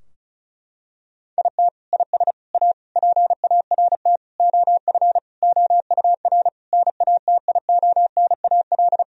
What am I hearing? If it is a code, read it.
Morse code, heard as a keyed tone, about 36 wpm, 700 Hz. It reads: IT IS A PART OF OUR NATIONAL
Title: Sentences from Top 1000 Words - Review Part 4 - 35wpm